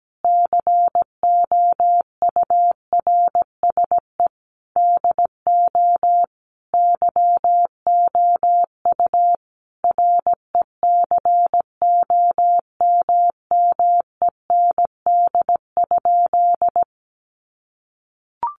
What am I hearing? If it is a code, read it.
Morse code, heard as a keyed tone, about 17 wpm, 700 Hz. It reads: COURSE DO YOU RECOMMEND?
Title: Sentences from Top 1200 Words 17wpm